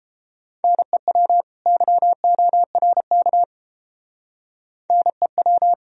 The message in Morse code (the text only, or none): NEW YORK NEW